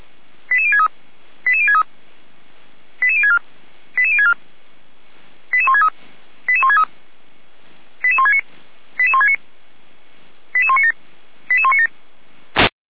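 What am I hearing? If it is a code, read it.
Five-tone selective call (ZVEI), heard as quick runs of five tones, tones of about 70 ms. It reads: 88063 88063 88064 88064 88163 88163 88169 88169 88197 88197
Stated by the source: ZVEI five-tone selective call received by SDR.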